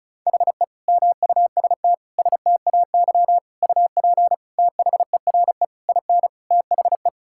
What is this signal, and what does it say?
Morse code, heard as a keyed tone, about 34 wpm, 700 Hz. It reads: HE MUST STAY UP THERE IN THE
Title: Sentences from Top 1600 Words 35wpm